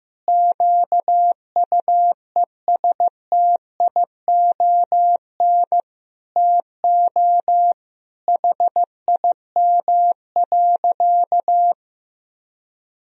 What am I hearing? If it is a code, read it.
Morse code, heard as a keyed tone, about 15 wpm, 700 Hz. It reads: QUESTION TO HIM.
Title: Sentences from Top 1000 Words - Review Part 2 - 15wpm